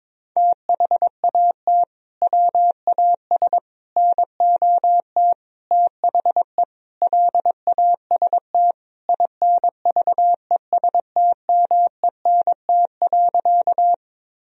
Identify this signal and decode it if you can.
Morse code, heard as a keyed tone, about 22 wpm, 700 Hz. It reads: THAT WAS NOT THE LAST INVESTMENT.
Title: Sentences from Top 1000 Words - Review Part 4 - 22wpm